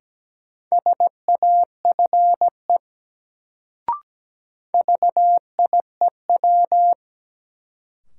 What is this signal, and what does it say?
Morse code, heard as a keyed tone, about 17 wpm, 700 Hz. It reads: SAFE VIEW
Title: Sentences from Top 900 Words - New Word Drill - 17wpm